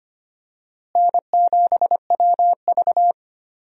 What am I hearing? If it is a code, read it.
Morse code, heard as a keyed tone, about 25 wpm, 700 Hz. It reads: N7WV
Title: USA Call Signs - 4 Characters - Repeated 2x - 25wpm